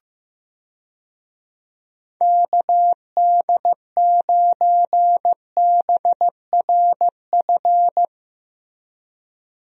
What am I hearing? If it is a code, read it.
Morse code, heard as a keyed tone, about 15 wpm, 700 Hz. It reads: KD9BRF
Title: USA Call Signs - 6 Characters - Repeated 2x - 15wpm